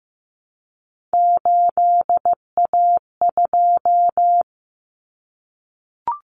Morse code, heard as a keyed tone, about 15 wpm, 700 Hz. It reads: 8A2
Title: Common Cut Numbers - 3-Digit 15wpm